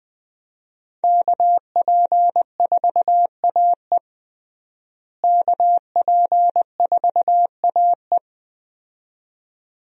Morse code, heard as a keyed tone, about 21 wpm, 700 Hz. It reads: KP4AE KP4AE
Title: USA Call Signs - 5 Characters - Repeated 2x - 20wpm